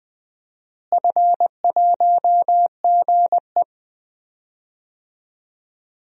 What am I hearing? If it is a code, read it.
Morse code, heard as a keyed tone, about 20 wpm, 700 Hz. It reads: F1GE
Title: DXCC-100 Call Signs - 4 Characters - Repeated 3x - 20wpm